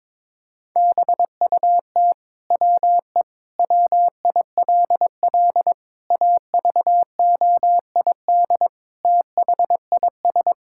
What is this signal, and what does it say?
Morse code, heard as a keyed tone, about 22 wpm, 700 Hz. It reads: BUT WE WILL AVOID THIS
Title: Sentences from Top 1000 Words - Review Part 6 - 22wpm